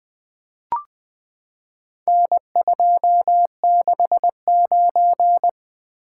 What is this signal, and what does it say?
Morse code, heard as a keyed tone, about 20 wpm, 700 Hz. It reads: N269
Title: Four Letter-Number Combinations - 20wpm